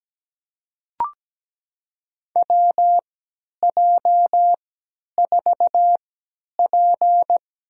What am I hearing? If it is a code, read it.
Morse code, heard as a keyed tone, about 17 wpm, spaced as for 10 wpm, 700 Hz. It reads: WJ4P
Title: USA Call Signs - 4 Characters - 3x Character Spacing - 17wpm